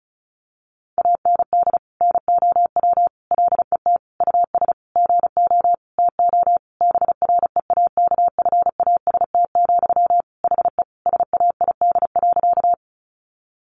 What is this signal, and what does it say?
Morse code, heard as a keyed tone, about 34 wpm, 700 Hz. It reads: AND NOW LET US GO TO BREAKFAST, HE SAID.
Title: Sentences from Top 1500 Words 35wpm